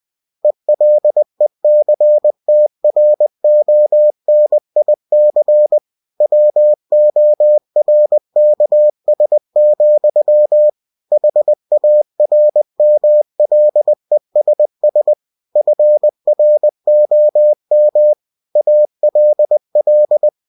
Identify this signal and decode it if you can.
Morse code, heard as a keyed tone, about 20 wpm, 600 Hz. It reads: ELECTRONIC WORKS, HARMLESS FROM ALL
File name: cw-books / winnie-the-pooh / winnie-20wpm-page-0120